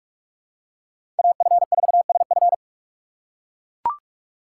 Morse code, heard as a keyed tone, about 44 wpm, 700 Hz. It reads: AF4SF
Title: USA Call Signs - 5 Characters - 3x Character Spacing - 45wpm